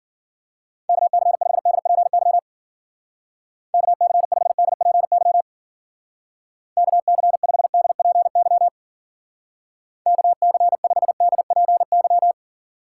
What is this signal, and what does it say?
Morse code, heard as a keyed tone, about 50 wpm, 700 Hz. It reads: KC5DPY KC5DPY KC5DPY KC5DPY
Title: Speed Racing - USA 2x3 Call Signs 40wpm